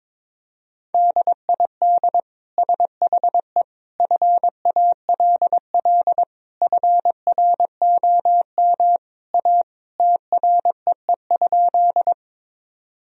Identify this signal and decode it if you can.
Morse code, heard as a keyed tone, about 21 wpm, 700 Hz. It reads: DID SHE FALL FROM A TREE?
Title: Sentences from Top 1000 Words - Review Part 2 - 22wpm